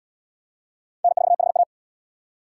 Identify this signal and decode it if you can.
Morse code, heard as a keyed tone, about 69 wpm, 700 Hz. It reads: N4HA